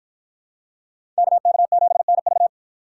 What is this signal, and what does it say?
Morse code, heard as a keyed tone, about 48 wpm, 700 Hz. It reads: KK7NV